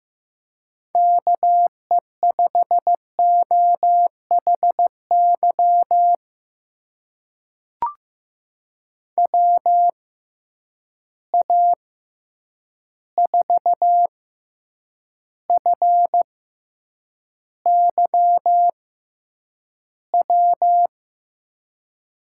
Morse code, heard as a keyed tone, about 15 wpm, spaced as for 8 wpm, 700 Hz. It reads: KE5OHY WA4FYW
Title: USA Call Signs - 6 Characters - 6x Character Spacing - 15wpm